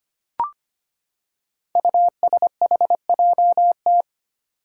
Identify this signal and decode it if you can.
Morse code, heard as a keyed tone, about 25 wpm, 700 Hz. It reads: USHJT
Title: Five Challenging Letter-Number Combinations 25wpm